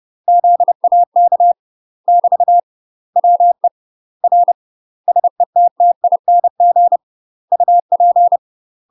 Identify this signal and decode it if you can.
Morse code, heard as a keyed tone, about 30 wpm, 700 Hz. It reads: ZAK <BT> WE R SETTING UP
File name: UK MASTER QSOs TRAINING / 30WPM / UK MASTER QSOs 0016